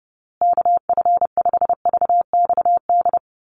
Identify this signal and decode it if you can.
Morse code, heard as a keyed tone, about 30 wpm, 700 Hz. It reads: KF5VXD
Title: USA Call Signs Repeated 3x 30wpm